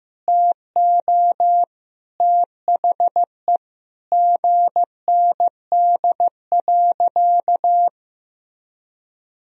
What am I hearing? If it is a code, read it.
Morse code, heard as a keyed tone, about 16 wpm, 700 Hz. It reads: TO THE GND.
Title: Sentences from Top 500 Words and Abbreviations 15wpm